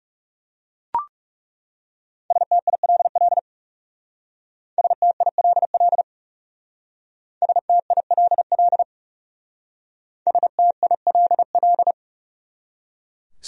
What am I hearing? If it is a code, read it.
Morse code, heard as a keyed tone, about 38 wpm, 700 Hz. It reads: STILL STILL STILL STILL